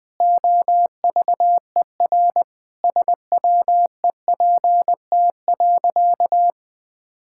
Morse code, heard as a keyed tone, about 20 wpm, 700 Hz. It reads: OVER SWEPT.
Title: All-Age Appropriate Jokes - 20wpm